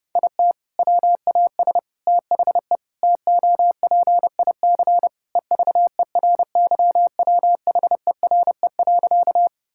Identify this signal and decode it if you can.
Morse code, heard as a keyed tone, about 32 wpm, 700 Hz. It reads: IT WAS THE TOPIC EVERYWHERE.
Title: Sentences from Top 1600 Words 30wpm